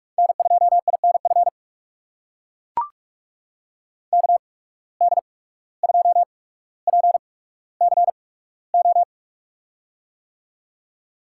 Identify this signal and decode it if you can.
Morse code, heard as a keyed tone, about 44 wpm, spaced as for 18 wpm, 700 Hz. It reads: N2INF KD2PCO
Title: USA Call Signs - 8x Character Spacing - 45wpm